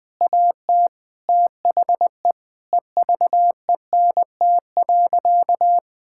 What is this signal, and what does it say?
Morse code, heard as a keyed tone, about 20 wpm, 700 Hz. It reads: AT THE EVENT.
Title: Sentences with 5-Word Phrases - Spoken Beginning - 20wpm